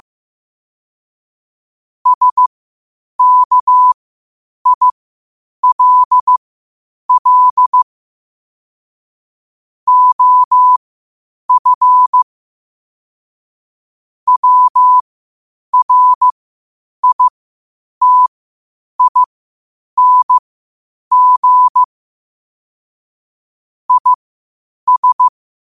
Morse code, heard as a keyed tone, about 15 wpm, spaced as for 8 wpm, 1000 Hz. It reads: SKILL OF WRITING IS
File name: qotdfiles / 8wpm